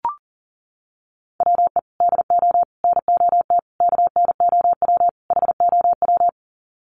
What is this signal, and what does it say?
Morse code, heard as a keyed tone, about 37 wpm, 700 Hz. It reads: WE DO NOT KNOW HOW
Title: Sets of 5 Words Encore 40wpm